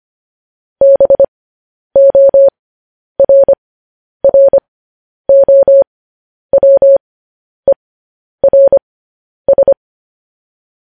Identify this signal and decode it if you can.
Morse code, heard as a keyed tone, about 25 wpm, spaced as for 10 wpm, 550 Hz. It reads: BORROWERS
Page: news headlines in morse code at 10 wpm